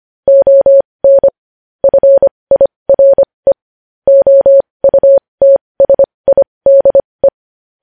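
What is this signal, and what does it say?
Morse code, heard as a keyed tone, about 25 wpm, spaced as for 20 wpm, 550 Hz. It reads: ON FIRE OUTSIDE
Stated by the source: Morse code news headlines practice at 20.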